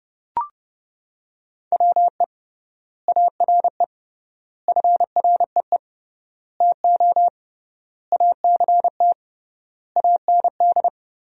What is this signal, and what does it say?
Morse code, heard as a keyed tone, about 32 wpm, 700 Hz. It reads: WE ARE FREE TO ACT AND
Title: 3x Word-Spacing - Sentences from Top 500 Words - 30wpm